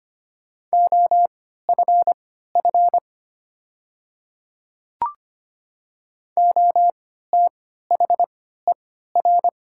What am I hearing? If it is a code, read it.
Morse code, heard as a keyed tone, about 25 wpm, spaced as for 14 wpm, 700 Hz. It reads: OFF OTHER
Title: 3x Character Spacing - Top 200 Words - 25wpm